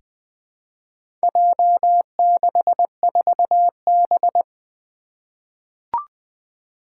Morse code, heard as a keyed tone, about 20 wpm, 700 Hz. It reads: J64B
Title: Four Challenging Letter-Number Combinations 20wpm